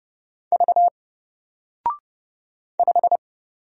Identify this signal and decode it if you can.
Morse code, heard as a keyed tone, about 30 wpm, 700 Hz. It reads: V 5
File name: Single-Letter-Number-30wpm